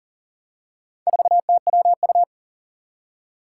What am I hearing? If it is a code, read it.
Morse code, heard as a keyed tone, about 40 wpm, 700 Hz. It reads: 4TWU
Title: Four Challenging Letter-Number Combinations 40wpm